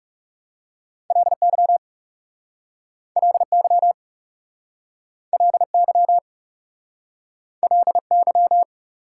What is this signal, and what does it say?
Morse code, heard as a keyed tone, about 37 wpm, 690 Hz. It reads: LY LY LY LY
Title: Speed-Racing Prefixes and Suffixes 30wpm